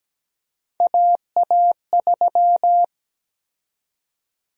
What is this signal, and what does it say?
Morse code, heard as a keyed tone, about 17 wpm, 700 Hz. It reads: AA3
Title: USA Call Sign Prefixes - 17wpm